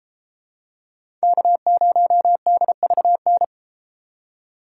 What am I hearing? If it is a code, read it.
Morse code, heard as a keyed tone, about 32 wpm, 700 Hz. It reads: K0DVN